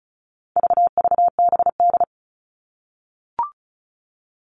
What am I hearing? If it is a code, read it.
Morse code, heard as a keyed tone, about 34 wpm, 700 Hz. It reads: VVBD
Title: All Cut Numbers - 4-Digit 35wpm